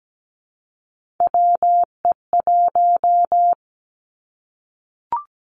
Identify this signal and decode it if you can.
Morse code, heard as a keyed tone, about 17 wpm, 700 Hz. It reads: WE1
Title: Three Challenging Letter-Number Combinations 17wpm